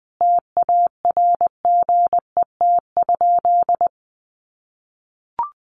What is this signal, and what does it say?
Morse code, heard as a keyed tone, about 20 wpm, 700 Hz. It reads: TARGET?
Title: Sentences from Top 1200 Words 20wpm